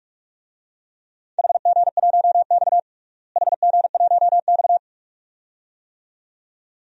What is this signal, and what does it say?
Morse code, heard as a keyed tone, about 41 wpm, 700 Hz. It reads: HG1X HG1X
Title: DXCC-100 Call Signs - 4 Characters - Repeated 2x - 45wpm